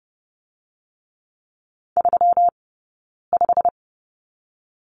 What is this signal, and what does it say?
Morse code, heard as a keyed tone, about 30 wpm, spaced as for 9 wpm, 700 Hz. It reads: 35
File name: Lesson 20d - Number 3 - Two Characters 30wpm